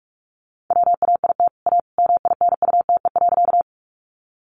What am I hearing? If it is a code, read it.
Morse code, heard as a keyed tone, about 44 wpm, 700 Hz. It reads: WAIT A MINUTE.
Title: Sentences from Top 500 Words 45wpm